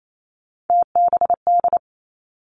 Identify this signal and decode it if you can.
Morse code, heard as a keyed tone, about 28 wpm, 700 Hz. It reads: TBD
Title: Three Challenging Letter-Number Combinations 28wpm